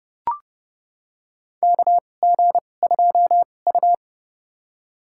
Morse code, heard as a keyed tone, about 32 wpm, spaced as for 22 wpm, 700 Hz. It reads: KG2U
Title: USA Call Signs - 4 Characters - 2x Character Spacing - 30wpm